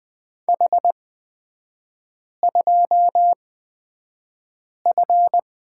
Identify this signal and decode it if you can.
Morse code, heard as a keyed tone, about 20 wpm, spaced as for 5 wpm, 700 Hz. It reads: H2F
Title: 224 Lesson 36e - BT - Three Characters 20wpm